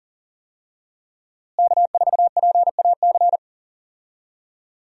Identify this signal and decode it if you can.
Morse code, heard as a keyed tone, about 40 wpm, 700 Hz. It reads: K4PAC
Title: USA Call Signs - 5 Characters - Repeated 3x - 40wpm